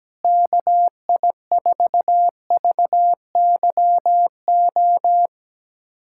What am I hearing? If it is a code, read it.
Morse code, heard as a keyed tone, about 17 wpm, 700 Hz. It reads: KI4VYO
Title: USA Callsign DE Callsign - 6 Characters - 17wpm